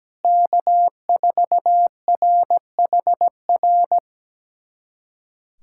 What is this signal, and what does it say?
Morse code, heard as a keyed tone, about 17 wpm, 700 Hz. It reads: K4RHR
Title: USA Callsign DE Callsign 17wpm